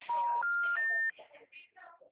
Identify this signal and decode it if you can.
Special information tone, with speech.